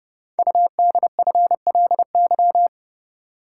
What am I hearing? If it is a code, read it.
Morse code, heard as a keyed tone, about 30 wpm, 690 Hz. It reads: UDFLY
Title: Code Groups - 5 Letters - 30wpm